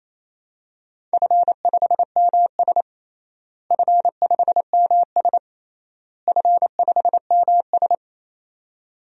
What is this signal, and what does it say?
Morse code, heard as a keyed tone, about 28 wpm, 700 Hz. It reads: F5MS F5MS F5MS
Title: DXCC-100 Call Signs - 4 Characters - Repeated 3x - 28wpm